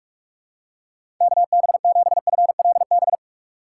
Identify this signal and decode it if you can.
Morse code, heard as a keyed tone, about 42 wpm, 700 Hz. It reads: KB7FLB
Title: USA Call Signs - 3x Character Spacing - 45wpm